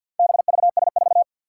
Morse code, heard as a keyed tone, about 52 wpm, 700 Hz. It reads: BVS4
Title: Four Challenging Letter-Number Combinations 50wpm